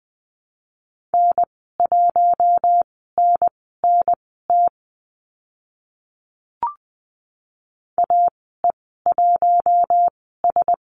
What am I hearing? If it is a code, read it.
Morse code, heard as a keyed tone, about 20 wpm, spaced as for 15 wpm, 700 Hz. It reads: N1NNT AE1S
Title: 042 Lesson 8g - Number 1 - Call Signs 20wpm